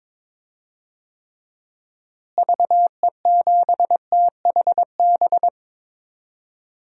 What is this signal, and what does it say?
Morse code, heard as a keyed tone, about 22 wpm, 700 Hz. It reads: VE7THB
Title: DXCC-100 Call Signs - 6 Characters - Repeated 2x - 22wpm